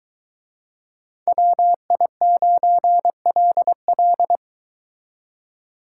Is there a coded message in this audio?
Yes: Morse code from 1.3 to 4.4 s, heard as a keyed tone, about 22 wpm, 700 Hz. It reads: WI9LL